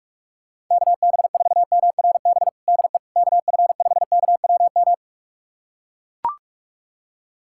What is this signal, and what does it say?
Morse code, heard as a keyed tone, about 44 wpm, 700 Hz. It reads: KB4MRB DE KF5KWM
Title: USA Callsign DE Callsign 45wpm